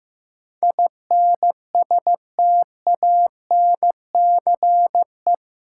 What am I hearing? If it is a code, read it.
Morse code, heard as a keyed tone, about 15 wpm, 700 Hz. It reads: INSTANCE
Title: Sentences from Top 800 Words - New Word Drill - 15wpm